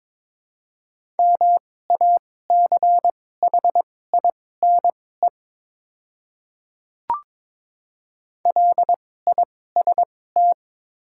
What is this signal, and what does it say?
Morse code, heard as a keyed tone, about 21 wpm, spaced as for 16 wpm, 700 Hz. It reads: MACHINE LIST
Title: 2x Character Spacing - Top 500 Words - 22wpm